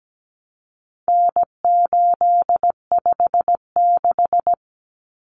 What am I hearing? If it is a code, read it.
Morse code, heard as a keyed tone, about 17 wpm, 700 Hz. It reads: N856
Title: Common Cut Numbers - 4-Digit 17wpm